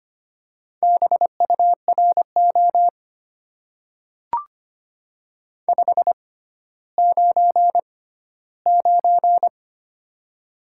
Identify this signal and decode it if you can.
Morse code, heard as a keyed tone, about 25 wpm, spaced as for 14 wpm, 690 Hz. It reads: BURO 599
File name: Top 90 QSO Elements - 6x Character Spacing - 25wpm